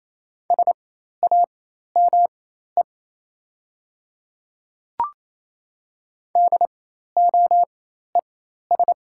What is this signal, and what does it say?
Morse code, heard as a keyed tone, about 28 wpm, spaced as for 13 wpm, 690 Hz. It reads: SAME DOES
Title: 4x Character Spacing - Top 200 Words - 28wpm